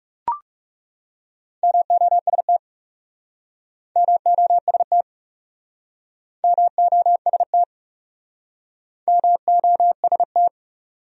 Morse code, heard as a keyed tone, about 39 wpm, 700 Hz. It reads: MOST MOST MOST MOST